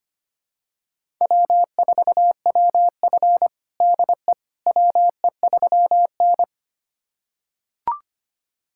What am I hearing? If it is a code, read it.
Morse code, heard as a keyed tone, about 25 wpm, 700 Hz. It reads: W4WF DE WE3N
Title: USA Callsign DE Callsign - 4 Characters - 25wpm